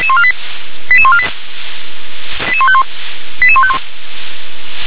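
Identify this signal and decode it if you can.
Five-tone selective call (ZVEI), heard as quick runs of five tones, tones of about 70 ms. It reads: 88148 88151 88151
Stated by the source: ZVEI five-tone call received by SDR.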